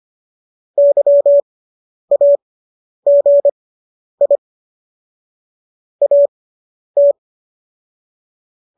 Morse code, heard as a keyed tone, about 25 wpm, spaced as for 10 wpm, 570 Hz. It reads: YAGI AT